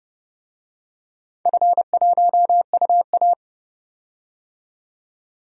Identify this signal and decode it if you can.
Morse code, heard as a keyed tone, about 30 wpm, 700 Hz. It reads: F1UA
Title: DXCC-100 Call Signs - 4 Characters - 30wpm